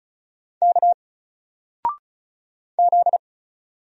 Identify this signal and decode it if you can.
Morse code, heard as a keyed tone, about 34 wpm, 700 Hz. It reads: K Z
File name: Single-Letter-Number-35wpm